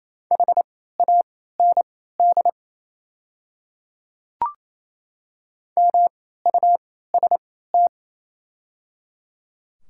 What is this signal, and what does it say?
Morse code, heard as a keyed tone, about 28 wpm, spaced as for 16 wpm, 700 Hz. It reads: HAND MUST
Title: 3x Character Spacing - Top 200 Words - 28wpm